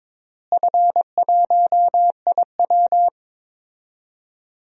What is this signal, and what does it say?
Morse code, heard as a keyed tone, about 21 wpm, 700 Hz. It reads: F1IW
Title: DXCC-100 Call Signs - 4 Characters - Repeated 3x - 22wpm